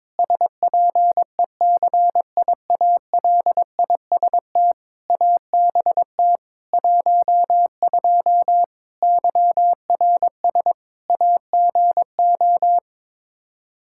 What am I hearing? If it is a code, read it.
Morse code, heard as a keyed tone, about 22 wpm, 700 Hz. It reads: SPECIALIST ABT 12 YRS AGO
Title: Practice QSO - Exchange 3 - More Info - 22wpm